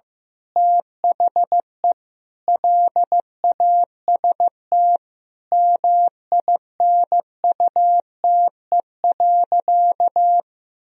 Morse code, heard as a keyed tone, about 16 wpm, 700 Hz. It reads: THE LAST MINUTE.
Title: Sentences from Top 1000 Words - Review Part 2 - 15wpm